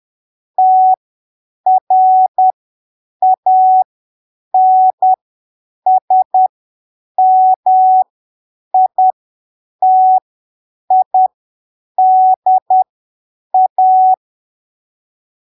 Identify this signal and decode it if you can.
Morse code, heard as a keyed tone, about 10 wpm, spaced as for 7 wpm, 750 Hz. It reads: TRANSMITIDA